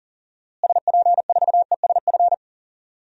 Morse code, heard as a keyed tone, about 40 wpm, 700 Hz. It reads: SP4ESF
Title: DXCC-100 Call Signs - 6 Characters - Repeated 2x - 40wpm